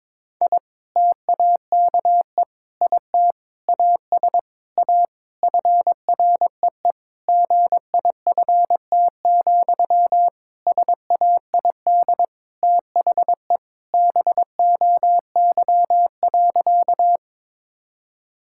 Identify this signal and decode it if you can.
Morse code, heard as a keyed tone, about 22 wpm, 700 Hz. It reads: I TAKE IT AS A FREE GIFT, SAID THE BOY.
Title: Sentences from Top 1000 Words - Review Part 6 - 22wpm